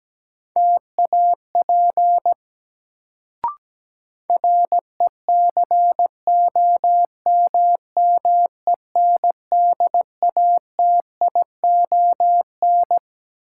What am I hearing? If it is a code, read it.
Morse code, heard as a keyed tone, about 17 wpm, 700 Hz. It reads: TAP RECOMMENDATION
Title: Sentences from Top 1400 Words - New Word Drill Alt - 17wpm